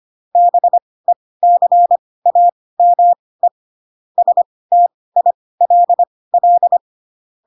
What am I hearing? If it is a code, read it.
Morse code, heard as a keyed tone, about 25 wpm, spaced as for 18 wpm, 700 Hz. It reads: BECAME STILL
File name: Pinocchio / Pinocchio ch1 12 25 ik1hge